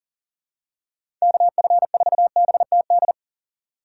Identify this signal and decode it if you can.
Morse code, heard as a keyed tone, about 40 wpm, 700 Hz. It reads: KF4BTD
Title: USA Call Signs - 6 Characters - 3x Character Spacing - 40wpm